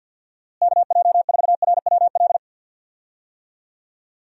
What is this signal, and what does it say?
Morse code, heard as a keyed tone, about 52 wpm, 700 Hz. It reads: KJ4RWL
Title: USA Call Signs - 6 Characters - Repeated 3x - 50wpm